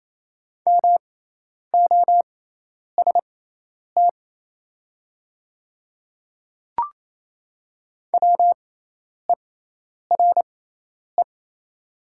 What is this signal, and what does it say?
Morse code, heard as a keyed tone, about 28 wpm, spaced as for 10 wpm, 700 Hz. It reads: MOST WERE
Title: 6x Character Spacing - Top 100 Words - 28wpm